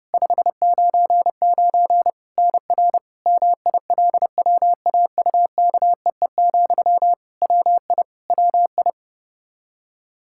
Morse code, heard as a keyed tone, about 30 wpm, 700 Hz. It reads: 599 NR MILWAUKEE, WI WI
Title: Practice QSO - Exchange 1 - Beginning - 30wpm